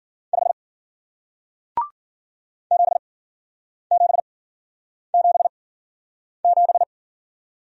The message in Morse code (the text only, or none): 5 7777